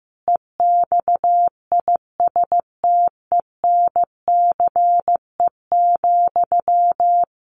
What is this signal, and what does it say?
Morse code, heard as a keyed tone, about 15 wpm, 700 Hz. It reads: EXISTENCE,
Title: A Sound Of Thunder 15wpm